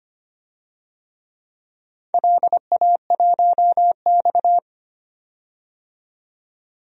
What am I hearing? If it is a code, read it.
Morse code, heard as a keyed tone, about 25 wpm, 700 Hz. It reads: LA1X
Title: DXCC-100 Call Signs - 4 Characters - 25wpm